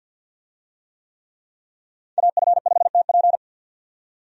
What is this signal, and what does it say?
Morse code, heard as a keyed tone, about 45 wpm, 700 Hz. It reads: AF5TP
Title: USA Call Signs - 5 Characters - 50wpm